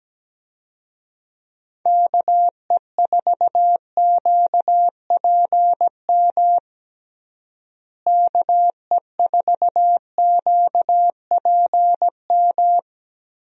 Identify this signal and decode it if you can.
Morse code, heard as a keyed tone, about 17 wpm, 700 Hz. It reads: KE4QPM KE4QPM
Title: USA Call Signs - 6 Characters - Repeated 3x - 17wpm